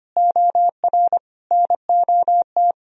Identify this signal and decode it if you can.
Morse code, heard as a keyed tone, about 31 wpm, 700 Hz. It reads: OR NOT